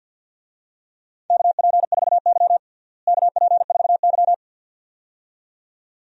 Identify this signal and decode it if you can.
Morse code, heard as a keyed tone, about 51 wpm, 700 Hz. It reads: KP4Y KP4Y
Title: DXCC-100 Call Signs - 4 Characters - Repeated 2x - 50wpm